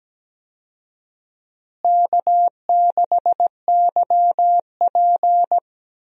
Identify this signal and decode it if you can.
Morse code, heard as a keyed tone, about 17 wpm, 700 Hz. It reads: K6YP
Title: USA Call Signs - 4 Characters - Repeated 2x - 17wpm